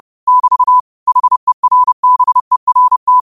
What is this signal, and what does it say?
Morse code, heard as a keyed tone, about 30 wpm, 1000 Hz. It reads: <BT> HERBERT